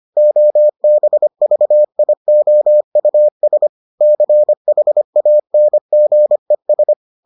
Morse code, heard as a keyed tone, about 25 wpm, 600 Hz. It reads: OBVIOUS CHANGES